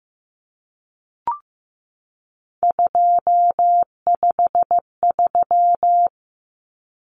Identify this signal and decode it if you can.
Morse code, heard as a keyed tone, about 15 wpm, 700 Hz. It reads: 253
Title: Numbers - 3-Digit 15wpm